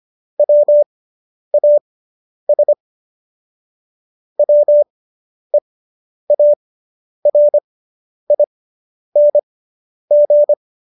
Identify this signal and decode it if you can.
Morse code, heard as a keyed tone, about 25 wpm, spaced as for 10 wpm, 590 Hz. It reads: WAS WEARING